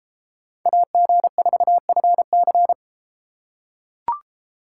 Morse code, heard as a keyed tone, about 32 wpm, 700 Hz. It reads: AG4FC